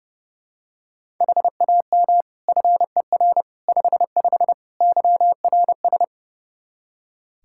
Morse code, heard as a keyed tone, about 31 wpm, 700 Hz. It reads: HAM FER 55 YRS
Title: Practice QSO - Exchange 3 - More Info - 30wpm